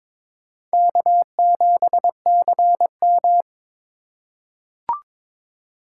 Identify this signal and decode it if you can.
Morse code, heard as a keyed tone, about 22 wpm, 700 Hz. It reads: K7CM
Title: USA Call Signs - 4 Characters - 3x Character Spacing - 22wpm